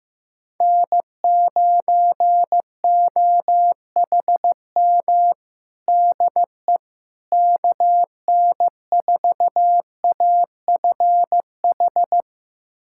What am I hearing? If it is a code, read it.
Morse code, heard as a keyed tone, about 15 wpm, 700 Hz. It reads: N9OHM DE KN4AFH
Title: USA Callsign DE Callsign 15wpm